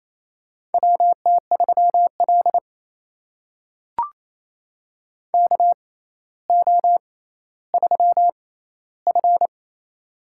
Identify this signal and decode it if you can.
Morse code, heard as a keyed tone, about 28 wpm, spaced as for 14 wpm, 700 Hz. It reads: WT3L KO3F
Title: USA Call Signs - 4 Characters - 6x Character Spacing - 28wpm